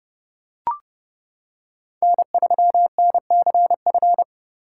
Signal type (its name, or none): Morse code